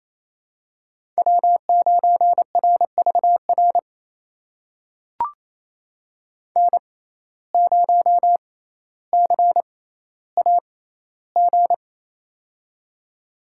Morse code, heard as a keyed tone, about 28 wpm, spaced as for 14 wpm, 700 Hz. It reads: W9RVR N0CAG
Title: USA Call Signs - 6x Character Spacing - 28wpm